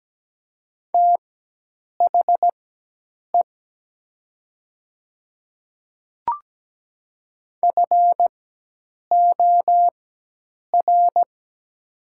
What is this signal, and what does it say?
Morse code, heard as a keyed tone, about 17 wpm, spaced as for 8 wpm, 700 Hz. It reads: THE FOR
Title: 4x Character Spacing - Top 300 Words - 17wpm